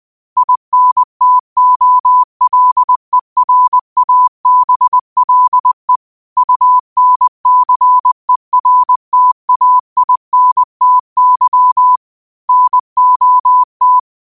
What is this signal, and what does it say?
Morse code, heard as a keyed tone, about 20 wpm, 1000 Hz. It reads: INTOLERABLE UNCERTAINTY NOT